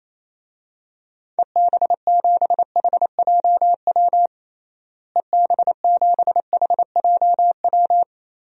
Morse code, heard as a keyed tone, about 28 wpm, 700 Hz. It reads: EB7HJW EB7HJW
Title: DXCC-100 Call Signs - 6 Characters - Repeated 2x - 28wpm